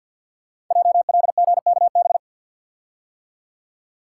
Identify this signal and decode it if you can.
Morse code, heard as a keyed tone, about 50 wpm, 700 Hz. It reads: JLGKB
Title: Code Groups - 5 Letters - 50wpm